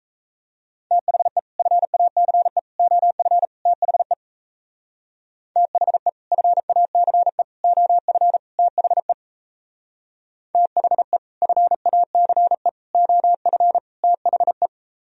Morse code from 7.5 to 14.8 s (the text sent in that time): OF THE THE FACE OF THE